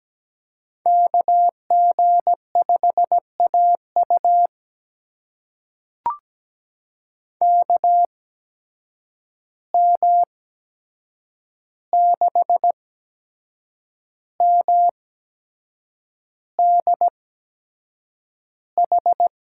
Morse code, heard as a keyed tone, about 17 wpm, spaced as for 7 wpm, 700 Hz. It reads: KG5AU KM6MDH
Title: USA Call Signs - 8x Character Spacing - 17wpm